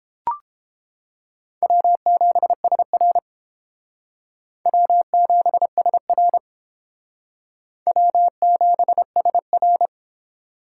Morse code, heard as a keyed tone, about 28 wpm, 700 Hz. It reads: W7SR W7SR W7SR